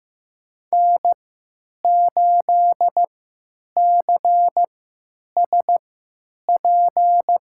Morse code, heard as a keyed tone, about 15 wpm, spaced as for 9 wpm, 700 Hz. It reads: N8CSP
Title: USA Call Signs - 3x Character Spacing - 15wpm